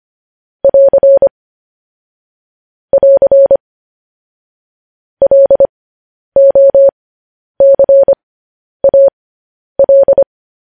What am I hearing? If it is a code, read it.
Morse code, heard as a keyed tone, about 25 wpm, spaced as for 10 wpm, 550 Hz. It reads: <AR> <AR> LOCAL
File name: podcast / News headlines-1x-25x10